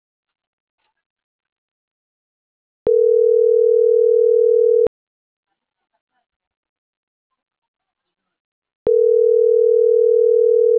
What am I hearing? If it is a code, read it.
Ringback tone (North America), 440 + 480 Hz.
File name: CENTRAL MAERSK / out adrian.moreno